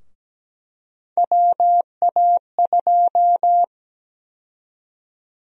Morse code, heard as a keyed tone, about 17 wpm, 700 Hz. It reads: WA2
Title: USA Call Sign Prefixes - Repeated 2x - 17wpm